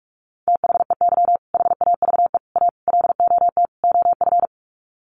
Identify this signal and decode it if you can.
Morse code, heard as a keyed tone, about 44 wpm, 700 Hz. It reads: THEY HAVE A LOT OF
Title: Sets of 5 Words Encore2 45wpm